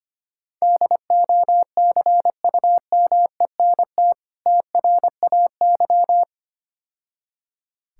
Morse code, heard as a keyed tone, about 25 wpm, 700 Hz. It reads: DOCUMENT TRAY
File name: practice / 25wpm / Difficult Words - Speed-Racing with Hints - 25wpm